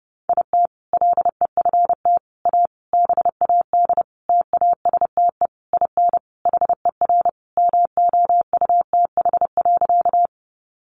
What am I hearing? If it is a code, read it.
Morse code, heard as a keyed tone, about 30 wpm, 700 Hz. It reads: IT LEFT A BAD TASTE IN HER MOUTH.
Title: Sentences from Top 1500 Words 30wpm